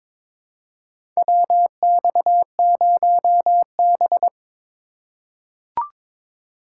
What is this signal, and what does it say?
Morse code, heard as a keyed tone, about 22 wpm, 700 Hz. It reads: WX0B
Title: USA Call Signs - 4 Characters - 3x Character Spacing - 22wpm